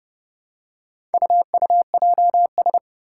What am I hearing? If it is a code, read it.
Morse code, heard as a keyed tone, about 30 wpm, 700 Hz. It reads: UUJS
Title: Four Challenging Letter-Number Combinations 30wpm